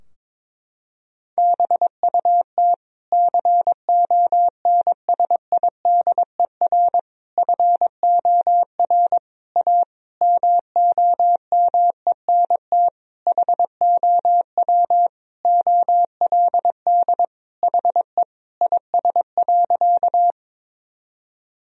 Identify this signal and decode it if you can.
Morse code, heard as a keyed tone, about 22 wpm, 700 Hz. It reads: BUT CONSIDER FOR A MOMENT HOW OLD HE IS.
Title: Sentences from Top 1000 Words - Review Part 4 - 22wpm